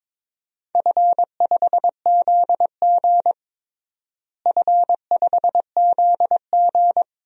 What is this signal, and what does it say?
Morse code, heard as a keyed tone, about 22 wpm, 700 Hz. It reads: F5ZG F5ZG
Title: DXCC-100 Call Signs - 4 Characters - Repeated 2x - 22wpm